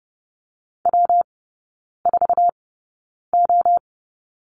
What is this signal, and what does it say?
Morse code, heard as a keyed tone, about 30 wpm, spaced as for 9 wpm, 700 Hz. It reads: W4O
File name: Lesson 19e - Letter w - Three Characters 30wpm